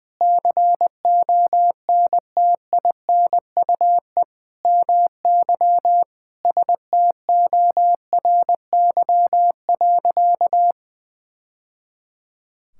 Morse code, heard as a keyed tone, about 20 wpm, 700 Hz. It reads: CONTINUE MY STORY.
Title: Sentences from Top 1000 Words - Review Part 6 - 20wpm